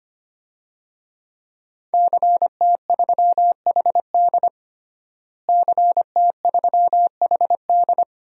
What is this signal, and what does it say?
Morse code, heard as a keyed tone, about 25 wpm, 700 Hz. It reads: CT3HD CT3HD
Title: DXCC-100 Call Signs - 5 Characters - Repeated 2x - 25wpm